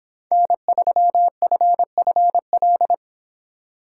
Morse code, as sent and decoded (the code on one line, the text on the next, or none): -. ...-- ..-. ..-. .-..
N3FFL